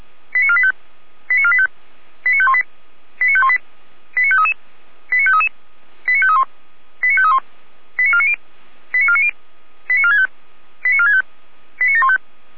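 Five-tone selective call (ZVEI), heard as quick runs of five tones, tones of about 70 ms. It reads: 89485 89485 89518 89518 89533 89533 89531 89531 89490 89490 89465 89465 89715
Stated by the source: ZVEI five-tone call received by SDR.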